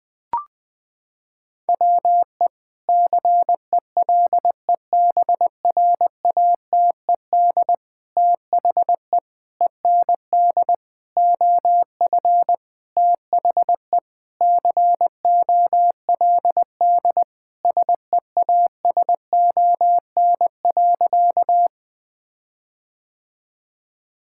Morse code, heard as a keyed tone, about 19 wpm, 700 Hz. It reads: WE CELEBRATED THE END OF THE COLD SEASON.
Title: Sentences with 5-Word Phrases - 20wpm